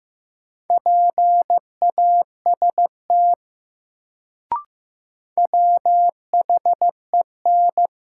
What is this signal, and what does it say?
Morse code, heard as a keyed tone, about 15 wpm, 700 Hz. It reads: PAST WHEN
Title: Sets of 2 Words - New Word Drill - 15wpm